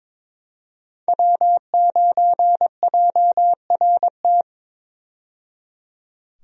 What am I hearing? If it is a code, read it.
Morse code, heard as a keyed tone, about 22 wpm, 700 Hz. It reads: W9JRT
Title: USA Call Signs - 5 Characters - 22wpm